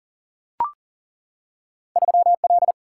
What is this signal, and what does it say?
Morse code, heard as a keyed tone, about 39 wpm, 700 Hz. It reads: 3L
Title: Single-Letter Single-Number 40wpm